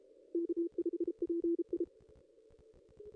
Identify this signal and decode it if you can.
Morse code, heard as a keyed tone, about 33 wpm, 340 Hz. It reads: K5PI
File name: contests / ss-cw-k5tr-op-n5rz -0100